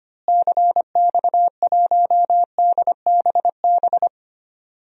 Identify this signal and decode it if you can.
Morse code, heard as a keyed tone, about 25 wpm, 700 Hz. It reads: CX1DBB
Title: DXCC-100 Call Signs - 6 Characters - Repeated 2x - 25wpm